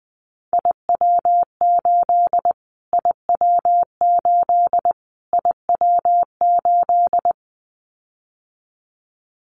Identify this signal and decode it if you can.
Morse code, heard as a keyed tone, about 20 wpm, 700 Hz. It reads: IW8 IW8 IW8
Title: Callsign Prefixes for Top 25 DXCC - Repeated 3x - 20wpm